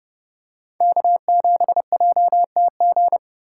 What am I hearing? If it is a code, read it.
Morse code, heard as a keyed tone, about 30 wpm, 700 Hz. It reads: K7JTG
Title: USA Callsign DE Callsign - 5 Characters - 30wpm